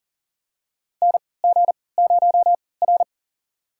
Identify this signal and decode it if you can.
Morse code, heard as a keyed tone, about 40 wpm, spaced as for 23 wpm, 700 Hz. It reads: NG0R
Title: USA Call Signs - 4 Characters - 3x Character Spacing - 40wpm